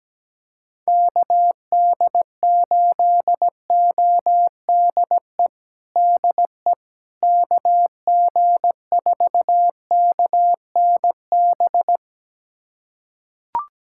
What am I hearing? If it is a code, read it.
Morse code, heard as a keyed tone, about 17 wpm, 700 Hz. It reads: KD8ODE DE KG4KNB
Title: USA Callsign DE Callsign - 6 Characters - 17wpm